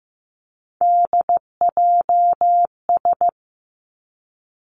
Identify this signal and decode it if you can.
Morse code, heard as a keyed tone, about 15 wpm, 700 Hz. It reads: DJS
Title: Three Challenging Letter-Number Combinations 15wpm